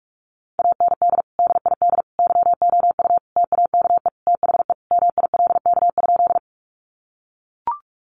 Morse code, heard as a keyed tone, about 44 wpm, 700 Hz. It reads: AND DID YOU TAKE THE MILK?
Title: Sentences from Top 1500 Words 45wpm